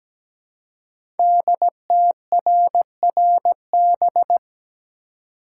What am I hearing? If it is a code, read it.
Morse code, heard as a keyed tone, about 17 wpm, 700 Hz. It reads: DTRRB
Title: Code Groups - 5 Letters - 17wpm